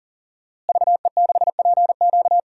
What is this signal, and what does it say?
Morse code, heard as a keyed tone, about 37 wpm, 700 Hz. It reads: VE6PQ
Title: DXCC-100 Call Signs - 5 Characters - Repeated 3x - 40wpm